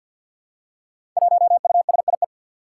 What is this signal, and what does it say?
Morse code, heard as a keyed tone, about 47 wpm, 700 Hz. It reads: 1USIE
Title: Five Challenging Letter-Number Combinations 50wpm